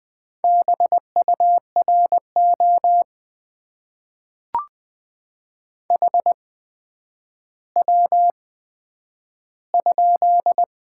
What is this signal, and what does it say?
Morse code, heard as a keyed tone, about 20 wpm, spaced as for 10 wpm, 700 Hz. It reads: BURO HW?
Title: Top 90 QSO Elements - 8x Character Spacing - 20wpm